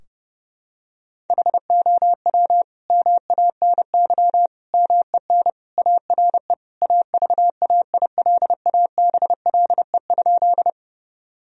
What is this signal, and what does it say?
Morse code, heard as a keyed tone, about 30 wpm, 700 Hz. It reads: HOW MANY MEN ARE AVAILABLE?
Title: Sentences from Top 1000 Words - Review Part 3 - 30wpm